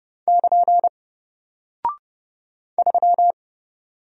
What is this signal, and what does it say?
Morse code, heard as a keyed tone, about 29 wpm, 700 Hz. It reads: <KN> 3
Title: Single Character (Letter:Number:Punctuation:Prosign) 30wpm